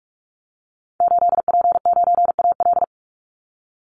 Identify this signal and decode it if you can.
Morse code, heard as a keyed tone, about 44 wpm, 700 Hz. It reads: 8P9AL
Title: Call Signs Repeated 3x - 45wpm